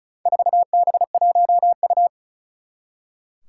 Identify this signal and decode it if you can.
Morse code, heard as a keyed tone, about 34 wpm, 700 Hz. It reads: 4B1U
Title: Four Challenging Letter-Number Combinations 35wpm